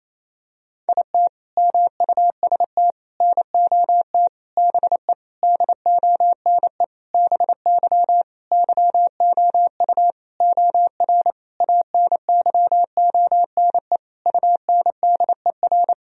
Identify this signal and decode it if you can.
Morse code, heard as a keyed tone, about 28 wpm, 700 Hz. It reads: IT MUST NOT BE DONE BY YOU OR ANYONE UNDER
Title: Sentences from Top 1000 Words - Review Part 1 - 28wpm